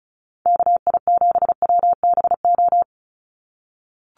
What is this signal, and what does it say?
Morse code, heard as a keyed tone, about 34 wpm, 700 Hz. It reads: KI7WBO
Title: USA Call Signs 35wpm